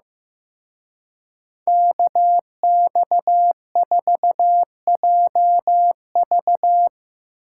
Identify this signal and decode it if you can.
Morse code, heard as a keyed tone, about 17 wpm, 700 Hz. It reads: KX4JV
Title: Speed Racing - USA 2x2 Call Signs 15wpm